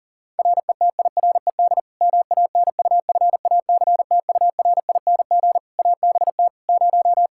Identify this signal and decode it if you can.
Morse code, heard as a keyed tone, about 40 wpm, 700 Hz. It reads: RETIRED MANUFACTURING ABT 0
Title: Practice QSO - Exchange 3 - More Info - 40wpm